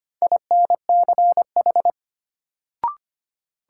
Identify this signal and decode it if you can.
Morse code, heard as a keyed tone, about 25 wpm, 700 Hz. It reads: INCH
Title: Sentences from Top 1000 Words - Review Part 3 - New Word Drill - 25wpm